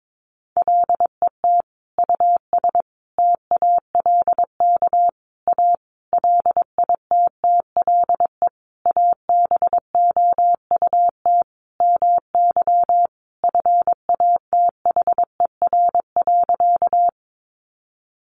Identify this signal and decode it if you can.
Morse code, heard as a keyed tone, about 22 wpm, 700 Hz. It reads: LET US TALK A LITTLE ABOUT MY FATHER.
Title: Sentences from Top 400 Words 22wpm